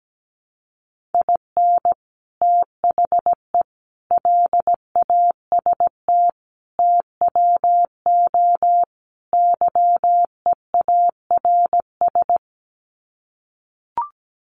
Morse code, heard as a keyed tone, about 17 wpm, 700 Hz. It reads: IN THE LAST TWO YEARS
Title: Sets of 5 Words 17wpm